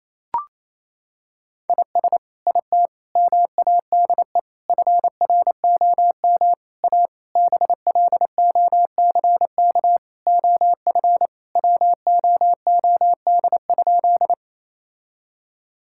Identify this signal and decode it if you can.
Morse code, heard as a keyed tone, about 28 wpm, 700 Hz. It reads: IS IT MADE FROM A BLOCK OF WOOD?
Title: Sentences from Top 1600 Words 28wpm